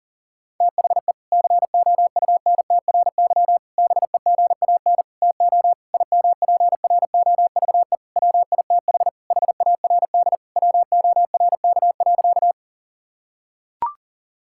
Morse code, heard as a keyed tone, about 40 wpm, 700 Hz. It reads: THE COUNTRY BEGAN TO IMPROVE WITH HARD WORK.
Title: Sentences from Top 1000 Words - Review Part 5 - 40wpm